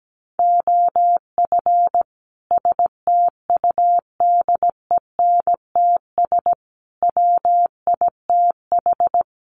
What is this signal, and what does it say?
Morse code, heard as a keyed tone, about 17 wpm, 700 Hz. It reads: OF STUDENTS WITH
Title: Sets of 5 Words Encore 17wpm